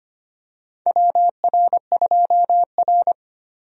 Morse code, heard as a keyed tone, about 25 wpm, 700 Hz. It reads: WR2R
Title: USA Call Signs - 4 Characters - 25wpm